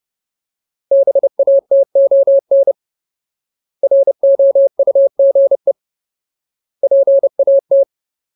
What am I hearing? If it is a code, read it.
Morse code, heard as a keyed tone, about 30 wpm, 550 Hz. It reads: BATON ROUGE PAT
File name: Ultimate Word List / 30WPM Ultimate